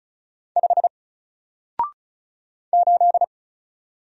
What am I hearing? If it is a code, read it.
Morse code, heard as a keyed tone, about 34 wpm, 700 Hz. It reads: 5 8
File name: Single-Letter-Number-35wpm